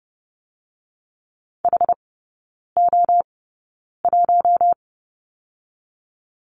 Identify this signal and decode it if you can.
Morse code, heard as a keyed tone, about 30 wpm, 700 Hz. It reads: H O 1
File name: Lesson 19e - Letter w - Three Characters 30wpm